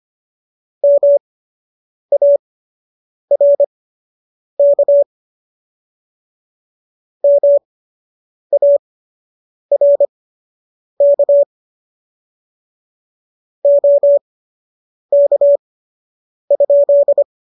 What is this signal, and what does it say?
Morse code, heard as a keyed tone, about 25 wpm, spaced as for 8 wpm, 570 Hz. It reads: MARK MARK OK?